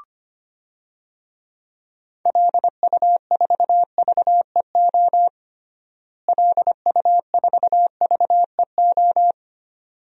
Morse code, heard as a keyed tone, about 25 wpm, 700 Hz. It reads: LU4VEO LU4VEO
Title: DXCC-100 Call Signs - 6 Characters - Repeated 2x - 25wpm